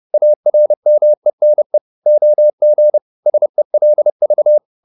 Morse code, heard as a keyed tone, about 30 wpm, 600 Hz. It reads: ARMENE OG SELV
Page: Ferie 30 wpm